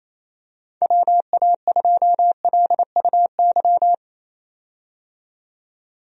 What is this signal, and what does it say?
Morse code, heard as a keyed tone, about 28 wpm, 700 Hz. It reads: WA2LUY
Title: USA Call Signs - 6 Characters - 28wpm